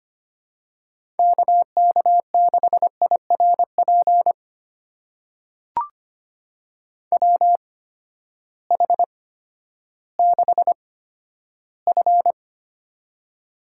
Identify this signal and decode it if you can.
Morse code, heard as a keyed tone, about 25 wpm, spaced as for 12 wpm, 700 Hz. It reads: KK6IRP WH6F
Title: USA Call Signs - 8x Character Spacing - 25wpm